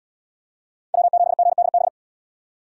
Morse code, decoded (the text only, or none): W7GDB